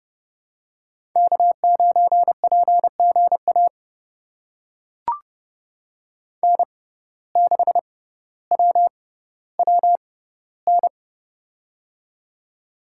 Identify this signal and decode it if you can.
Morse code, heard as a keyed tone, about 30 wpm, spaced as for 15 wpm, 700 Hz. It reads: K9PGA N6WWN
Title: USA Call Signs - 5 Characters - 6x Character Spacing - 30wpm